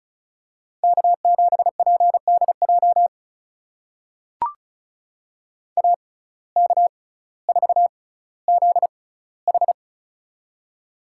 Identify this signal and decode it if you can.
Morse code, heard as a keyed tone, about 36 wpm, spaced as for 18 wpm, 700 Hz. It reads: K7PDJ AK4ZH
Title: USA Call Signs - 5 Characters - 6x Character Spacing - 35wpm